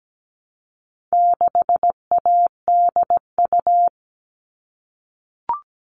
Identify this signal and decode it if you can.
Morse code, heard as a keyed tone, about 17 wpm, 700 Hz. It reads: 6ADU
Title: All Cut Numbers - 4-Digit 17wpm